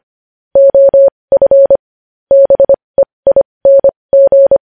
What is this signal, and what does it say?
Morse code, heard as a keyed tone, about 25 wpm, spaced as for 20 wpm, 550 Hz. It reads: OF BEING
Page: News Headlines in Morse Code at 20 WPM